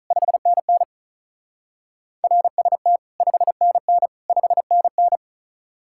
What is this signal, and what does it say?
Morse code, heard as a keyed tone, about 43 wpm, 700 Hz. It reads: 5NN RST 5NN 5NN
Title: Speed-Racing Common QSO Phrases Encore - 35wpm